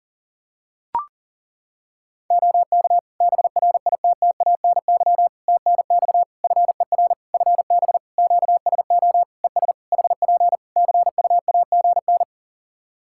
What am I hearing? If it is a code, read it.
Morse code, heard as a keyed tone, about 37 wpm, 700 Hz. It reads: OK BRITTANY TNX FER FB QSO ES HP CUAGN
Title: Practice QSO - Exchange 4 - Ending - 40wpm